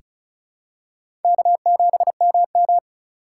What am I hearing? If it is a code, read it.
Morse code, heard as a keyed tone, about 36 wpm, 700 Hz. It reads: K7MM
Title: USA Call Signs - 3x Character Spacing - 35wpm